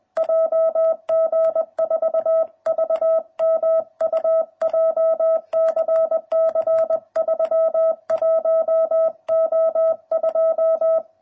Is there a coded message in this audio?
Yes: Morse code from 0.2 to 11.0 s, heard as a keyed tone, about 20 wpm, 660 Hz. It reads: JG4VMU JCC 31O2